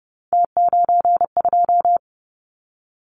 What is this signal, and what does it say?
Morse code, heard as a keyed tone, about 30 wpm, 700 Hz. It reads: T92
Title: Callsign Prefixes for Top 50 DXCC - 30wpm